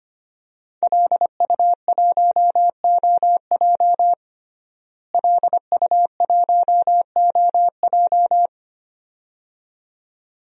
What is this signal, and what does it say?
Morse code, heard as a keyed tone, about 25 wpm, 700 Hz. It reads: LU1OJ LU1OJ
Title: DXCC-100 Call Signs - 5 Characters - Repeated 2x - 25wpm